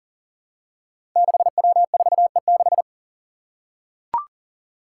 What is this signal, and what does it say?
Morse code, heard as a keyed tone, about 40 wpm, 700 Hz. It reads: 6W4E6
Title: Five Challenging Letter-Number Combinations 40wpm